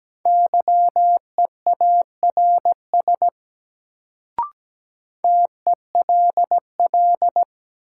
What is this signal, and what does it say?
Morse code, heard as a keyed tone, about 17 wpm, 700 Hz. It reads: YEARS TELL
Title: Sentences from Top 200 Words - New Word Drill - 17wpm